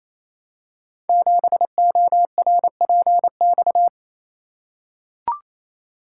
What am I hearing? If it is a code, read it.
Morse code, heard as a keyed tone, about 29 wpm, 700 Hz. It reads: 7ORPX
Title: Code Groups - 5 Letter:Numbers - 28wpm